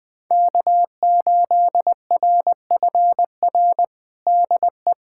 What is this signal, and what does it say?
Morse code, heard as a keyed tone, about 20 wpm, 700 Hz. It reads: K8RFR DE
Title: USA Callsign DE Callsign - 5 Characters - 20wpm